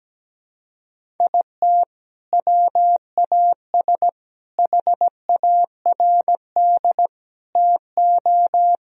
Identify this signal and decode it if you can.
Morse code, heard as a keyed tone, about 17 wpm, 700 Hz. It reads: IT WAS HARD TO
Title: Sentences from Top 1000 Words - Review Part 1 - 17wpm